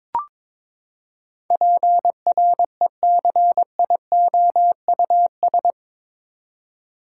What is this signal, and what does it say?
Morse code, heard as a keyed tone, about 22 wpm, 700 Hz. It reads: PRECIOUS
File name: practice / 22wpm / Eight Letter Words - 22wpm